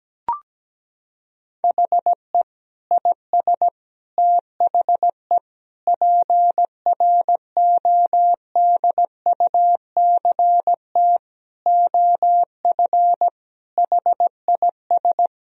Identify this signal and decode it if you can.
Morse code, heard as a keyed tone, about 17 wpm, 700 Hz. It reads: HE IS THE PRODUCT OF HIS
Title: Sentences from Top 1000 Words - Review Part 2 - 17wpm